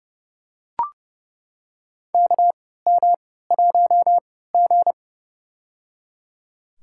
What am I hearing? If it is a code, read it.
Morse code, heard as a keyed tone, about 30 wpm, spaced as for 17 wpm, 700 Hz. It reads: KM1G
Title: USA Call Signs - 4 Characters - 3x Character Spacing - 30wpm